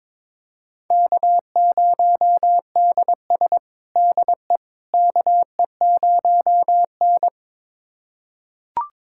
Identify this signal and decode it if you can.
Morse code, heard as a keyed tone, about 22 wpm, 700 Hz. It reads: K0DS DE KE0N
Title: USA Callsign DE Callsign - 4 Characters - 22wpm